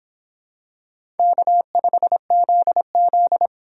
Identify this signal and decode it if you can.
Morse code, heard as a keyed tone, about 26 wpm, 700 Hz. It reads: K5ZZ